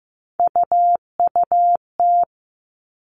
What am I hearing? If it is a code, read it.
Morse code, heard as a keyed tone, about 15 wpm, 700 Hz. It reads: UUT
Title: All Cut Numbers - 3-Digit 15wpm